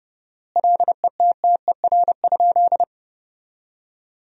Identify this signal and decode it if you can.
Morse code, heard as a keyed tone, about 30 wpm, 700 Hz. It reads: LETTER?
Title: 4x Word-Spacing - Sentences from Top 300 Words - 30wpm